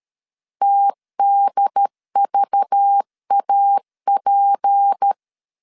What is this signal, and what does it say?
Morse code, heard as a keyed tone, about 13 wpm, 800 Hz. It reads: TDVAP
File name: CW / DET / 1 6 1500C S B 12wpm 3431